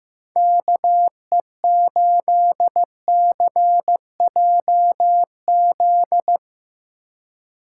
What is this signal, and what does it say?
Morse code, heard as a keyed tone, about 17 wpm, 700 Hz. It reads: KE8CJZ
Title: Speed Racing - USA 2x3 Call Signs 15wpm